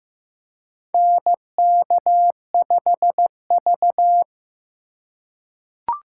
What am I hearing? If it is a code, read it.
Morse code, heard as a keyed tone, about 15 wpm, 700 Hz. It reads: NK5V
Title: USA Call Signs - 4 Characters - 3x Character Spacing - 15wpm